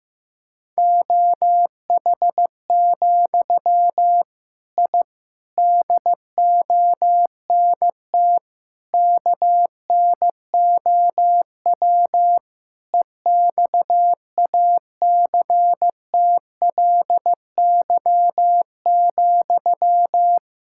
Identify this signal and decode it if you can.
Morse code, heard as a keyed tone, about 15 wpm, 700 Hz. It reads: OH, I DONT KNOW EXACTLY,
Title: Sentences from Top 1000 Words - Review Part 6 - 15wpm